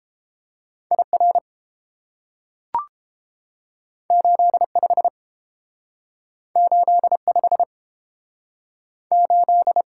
Morse code, heard as a keyed tone, about 30 wpm, 700 Hz. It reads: IR 85 85 8